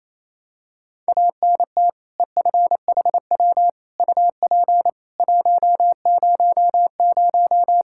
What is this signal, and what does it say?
Morse code, heard as a keyed tone, about 28 wpm, 700 Hz. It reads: ANT EFHW UP 100
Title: Practice QSO - Exchange 2 - Info - 28wpm